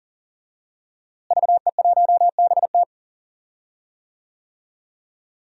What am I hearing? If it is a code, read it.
Morse code, heard as a keyed tone, about 40 wpm, 700 Hz. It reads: VE1BT
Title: DXCC-100 Call Signs - 5 Characters - Repeated 2x - 40wpm